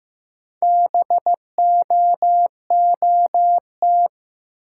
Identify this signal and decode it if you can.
Morse code, heard as a keyed tone, about 15 wpm, 700 Hz. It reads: BOOT
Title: Four Letter Words - 15wpm